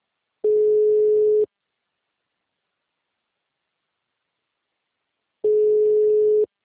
Ringback tone (Europe, 425 Hz).